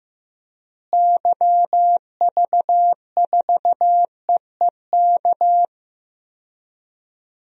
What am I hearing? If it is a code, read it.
Morse code, heard as a keyed tone, about 15 wpm, 700 Hz. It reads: YV4EEK
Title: DXCC-100 Call Signs - 6 Characters - Repeated 2x - 15wpm